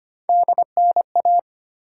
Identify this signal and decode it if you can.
Morse code, heard as a keyed tone, about 25 wpm, 700 Hz. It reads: DNA
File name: Three Letter-Number Combinations - 25wpm